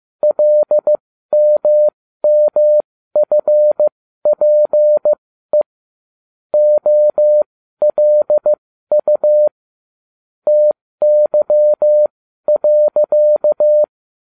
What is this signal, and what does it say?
Morse code, heard as a keyed tone, about 15 wpm, spaced as for 12 wpm, 600 Hz. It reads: LMMFPE OLU TY.